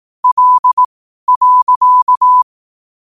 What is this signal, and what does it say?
Morse code, heard as a keyed tone, about 18 wpm, spaced as for 12 wpm, 1000 Hz. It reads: L.
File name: qotdfiles / 12wpm